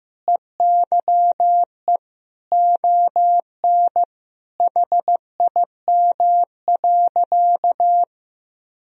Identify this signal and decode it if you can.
Morse code, heard as a keyed tone, about 15 wpm, 700 Hz. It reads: EYE ON HIM.
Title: Sentences from Top 1000 Words - Review Part 1 - 15wpm